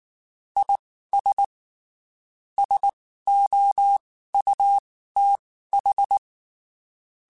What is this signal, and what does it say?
Morse code, heard as a keyed tone, about 19 wpm, spaced as for 14 wpm, 800 Hz. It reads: IS SOUTH
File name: (spaced) / QSOs Spaced-mp3s / kc7sac @18 sp